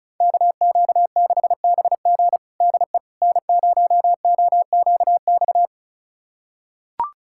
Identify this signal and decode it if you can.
Morse code, heard as a keyed tone, about 36 wpm, 700 Hz. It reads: KQ6BG DE N0OQX
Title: USA Callsign DE Callsign - 5 Characters - 35wpm